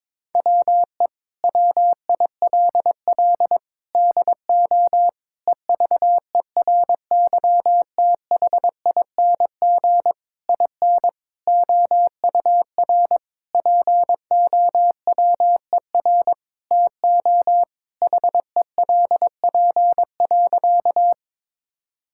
Morse code, heard as a keyed tone, about 21 wpm, 700 Hz. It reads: WE WILL DO EVERYTHING IN OUR POWER TO HELP.
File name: practice / 22wpm / Sentences with 8-Word Phrases - 22wpm